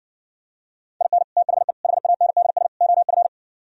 Morse code, heard as a keyed tone, about 59 wpm, 700 Hz. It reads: IN THE HANDS OF